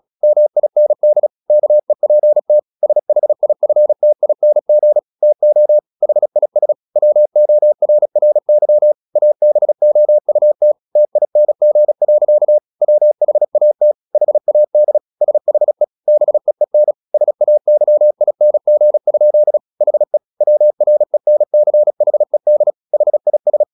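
Morse code, heard as a keyed tone, about 36 wpm, 600 Hz. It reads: MIND KEPT SHIFTING TO HIS WORRY ABOUT TING. WHAT HAD SHE BEEN SAYING? HE WRENCHED HIS
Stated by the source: Morse practice at 35 wpm from CW med Gnister, sending Ken Follett's novel Never in English.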